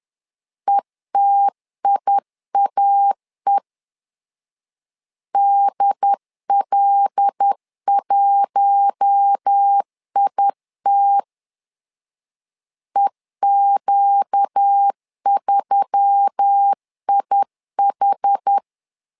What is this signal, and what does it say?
Morse code, heard as a keyed tone, about 10 wpm, 800 Hz. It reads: ETIAE DL1IT EQ3IH
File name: LEZIONI / 5 LBVC / 5 L 10wpm 2259